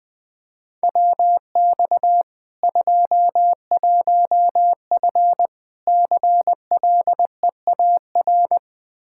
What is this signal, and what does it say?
Morse code, heard as a keyed tone, about 20 wpm, 700 Hz. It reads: WX 21F CLEAR
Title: K1USN SST QSO Practice 20wpm